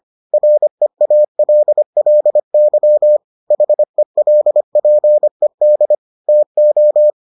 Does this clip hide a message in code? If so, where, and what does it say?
Yes: Morse code from 0.3 to 7.1 s, heard as a keyed tone, about 25 wpm, 600 Hz. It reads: REALLY HELPED TO